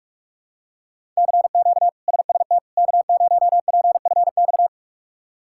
Morse code, heard as a keyed tone, about 43 wpm, 700 Hz. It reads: CQ SST K0PFX
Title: K1USN SST QSO Practice - 45wpm